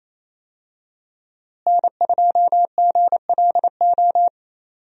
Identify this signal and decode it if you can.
Morse code, heard as a keyed tone, about 28 wpm, 700 Hz. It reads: N2GLO
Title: USA Call Signs - 5 Characters - Repeated 2x - 28wpm